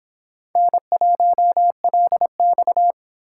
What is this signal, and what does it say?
Morse code, heard as a keyed tone, about 26 wpm, 700 Hz. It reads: N1LX